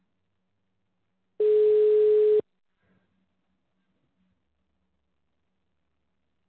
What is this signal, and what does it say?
Ringback tone (Europe, 425 Hz).